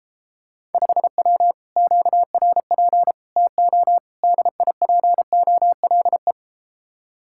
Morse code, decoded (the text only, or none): 5W QRP TO DIPOLE